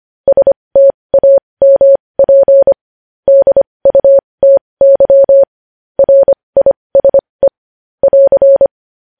Morse code, heard as a keyed tone, about 25 wpm, spaced as for 20 wpm, 550 Hz. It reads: STAMP DUTY RISE <AR>
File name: podcast / News headlines-1x-25x20